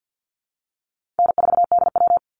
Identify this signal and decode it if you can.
Morse code, heard as a keyed tone, about 48 wpm, 700 Hz. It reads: N4DW
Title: Call Signs Repeated 3x - 50wpm